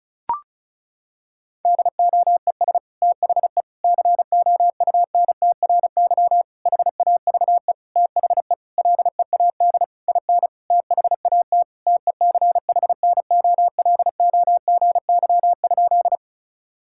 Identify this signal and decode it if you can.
Morse code, heard as a keyed tone, about 36 wpm, 700 Hz. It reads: DOES THE COUNTRY HAVE THE LEAD IN THAT TECHNOLOGY?
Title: Sentences from Top 1000 Words - Review Part 5 - 35wpm